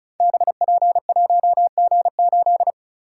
Morse code, heard as a keyed tone, about 33 wpm, 700 Hz. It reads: BP1G8
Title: Code Groups - 5 Letter:Numbers - 35wpm